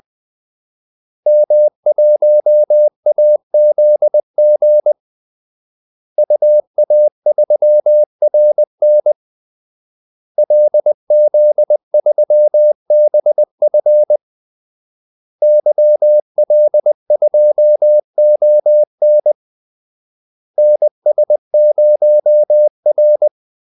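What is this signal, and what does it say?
Morse code, heard as a keyed tone, about 20 wpm, 600 Hz. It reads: M1AZG UA3RN LZ3BF YL2ON NS0R
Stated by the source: Morse call-sign practice at 20 wpm.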